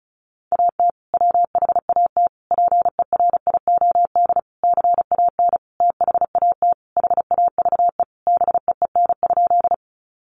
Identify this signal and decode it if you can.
Morse code, heard as a keyed tone, about 34 wpm, 700 Hz. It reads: AT WHAT PERIOD CAN THAT HAVE BEEN?
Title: Sentences from Top 800 Words 35wpm